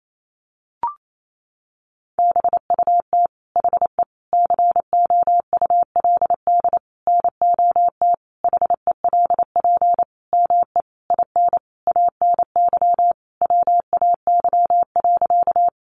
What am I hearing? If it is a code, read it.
Morse code, heard as a keyed tone, about 28 wpm, 700 Hz. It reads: BUT HE COULD NOT HELP ME IN ANY WAY.
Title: Sentences from Top 600 Words 28wpm